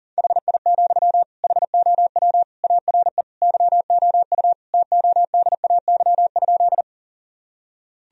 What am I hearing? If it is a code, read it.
Morse code, heard as a keyed tone, about 40 wpm, 700 Hz. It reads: HI, HOW ARE YOU TODAY?
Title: Sentences from Top 1000 Words - Review Part 3 - 40wpm